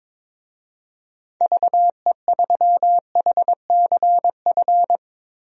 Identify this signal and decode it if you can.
Morse code, heard as a keyed tone, about 21 wpm, 700 Hz. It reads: VE3HCF
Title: DXCC-100 Call Signs - 6 Characters - Repeated 3x - 22wpm